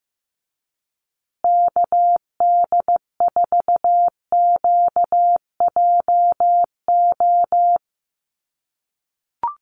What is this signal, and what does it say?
Morse code, heard as a keyed tone, about 15 wpm, 700 Hz. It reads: KD4QJO
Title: USA Call Signs Repeated 2x 15wpm